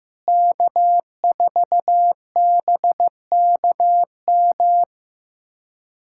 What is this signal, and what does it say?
Morse code, heard as a keyed tone, about 16 wpm, 700 Hz. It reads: K4BKM
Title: USA Callsign DE Callsign - 5 Characters - 15wpm